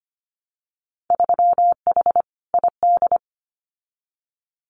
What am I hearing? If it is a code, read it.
Morse code, heard as a keyed tone, about 25 wpm, 700 Hz. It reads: 3H ID
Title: Winter Field Day Exchanges Encore 25wpm